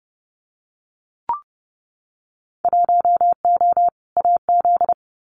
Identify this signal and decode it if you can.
Morse code, heard as a keyed tone, about 30 wpm, 700 Hz. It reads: 1O AZ
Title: Winter Field Day Exchanges 30wpm